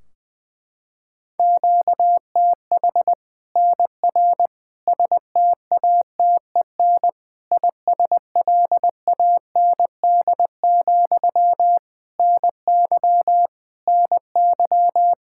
Morse code, heard as a keyed tone, about 21 wpm, 700 Hz. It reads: QTH NR STATEN ISLAND, NY NY
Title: Practice QSO - Exchange 1 - Beginning - 20wpm